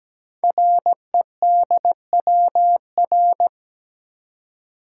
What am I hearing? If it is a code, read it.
Morse code, heard as a keyed tone, about 17 wpm, 700 Hz. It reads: REDWR
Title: Code Groups - 5 Letters - 17wpm